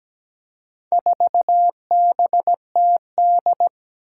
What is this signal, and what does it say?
Morse code, heard as a keyed tone, about 17 wpm, 700 Hz. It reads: 4BTD
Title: Four Challenging Letter-Number Combinations 17wpm